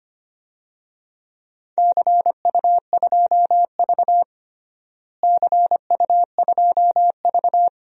Morse code, heard as a keyed tone, about 25 wpm, 700 Hz. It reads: CU2V CU2V
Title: DXCC-100 Call Signs - 4 Characters - Repeated 2x - 25wpm